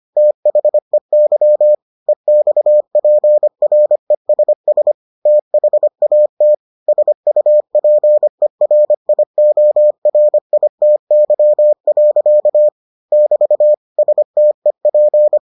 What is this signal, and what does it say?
Morse code, heard as a keyed tone, about 25 wpm, 600 Hz. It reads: THEY EXPRESS THAT SUPERIORITY. <BT> STEP